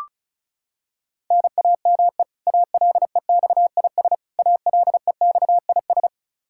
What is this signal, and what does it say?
Morse code, heard as a keyed tone, about 36 wpm, 700 Hz. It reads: NAME ALEXIS ALEXIS
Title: K1USN SST QSO Practice 35wpm